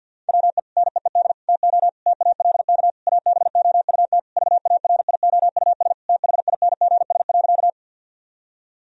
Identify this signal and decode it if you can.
Morse code, heard as a keyed tone, about 50 wpm, 700 Hz. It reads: WE NEED TO TALK ABOUT VARIOUS THINGS.
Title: Sentences from Top 1000 Words - Review Part 4 - 50wpm